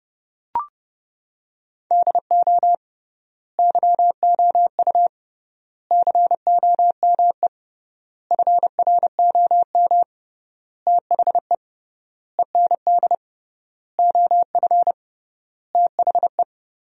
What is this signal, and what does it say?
Morse code, heard as a keyed tone, about 30 wpm, 700 Hz. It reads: DO YOU COME FROM THE END OF THE
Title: 3x Word-Spacing - Sentences from Top 200 Words - 30wpm